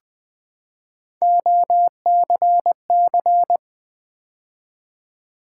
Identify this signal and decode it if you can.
Morse code, heard as a keyed tone, about 21 wpm, 700 Hz. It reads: OCC
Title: Build A Word - 3 Letters At A Time - 8-Letter Words - 20wpm